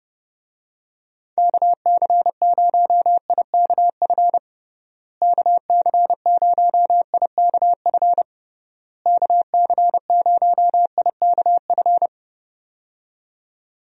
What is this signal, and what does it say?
Morse code, heard as a keyed tone, about 30 wpm, 700 Hz. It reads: KC0IKF KC0IKF KC0IKF
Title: USA Call Signs - 6 Characters - Repeated 3x - 30wpm